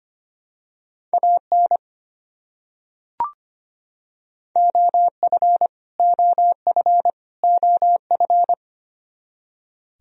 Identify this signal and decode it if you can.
Morse code, heard as a keyed tone, about 25 wpm, 700 Hz. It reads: AN OF OF OF
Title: Top 100 Words Repeated 25wpm